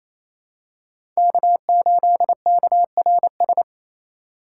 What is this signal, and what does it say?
Morse code, heard as a keyed tone, about 27 wpm, 700 Hz. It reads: K8KRS
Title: USA Call Signs - 5 Characters - 2x Character Spacing - 28wpm